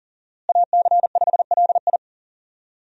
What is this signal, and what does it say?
Morse code, heard as a keyed tone, about 40 wpm, 700 Hz. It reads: AC5LI
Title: USA Call Signs - 5 Characters - Repeated 2x - 40wpm